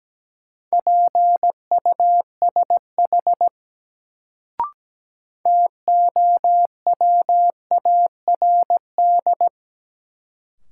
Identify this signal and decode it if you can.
Morse code, heard as a keyed tone, about 17 wpm, 700 Hz. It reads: PUSH TOWARD
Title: Sentences from Top 1300 Words - New Word Drill Alt - 17wpm